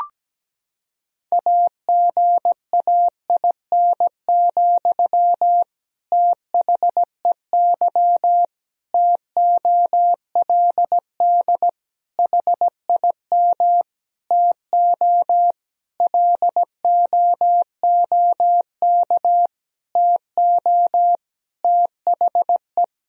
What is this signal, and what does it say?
Morse code, heard as a keyed tone, about 17 wpm, 700 Hz. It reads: AGAIN, THEY TOLD HIM TO LOOK TO THE
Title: Sentences from Top 1000 Words - Review Part 3 - 17wpm